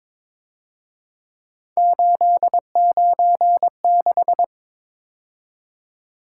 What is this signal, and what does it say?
Morse code, heard as a keyed tone, about 26 wpm, 700 Hz. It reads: 896
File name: Speed-Racing - Three Letter-Number Combinations - 22wpm